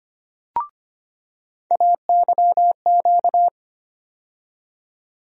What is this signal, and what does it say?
Morse code, heard as a keyed tone, about 25 wpm, 700 Hz. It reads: AYQ
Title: USA Call Sign Postfixes - 3 Letters 25wpm